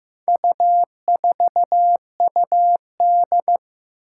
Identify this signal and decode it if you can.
Morse code, heard as a keyed tone, about 15 wpm, 700 Hz. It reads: U4UD
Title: Four Challenging Letter-Number Combinations 15wpm